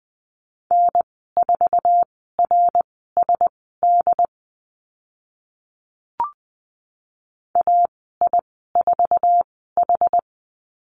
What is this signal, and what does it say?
Morse code, heard as a keyed tone, about 20 wpm, spaced as for 14 wpm, 700 Hz. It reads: N4RSD AI4H
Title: 069 Lesson 12g - Letter d - Call Signs 20wpm